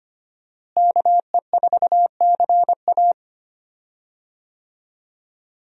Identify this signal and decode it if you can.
Morse code, heard as a keyed tone, about 25 wpm, 700 Hz. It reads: KE4CA
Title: USA Call Signs - 5 Characters - Repeated 2x - 25wpm